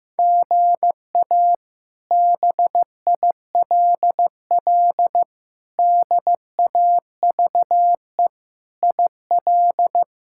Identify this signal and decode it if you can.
Morse code, heard as a keyed tone, about 15 wpm, 700 Hz. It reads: GA BILL DAVE IL
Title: K1USN SST QSO Practice - 15wpm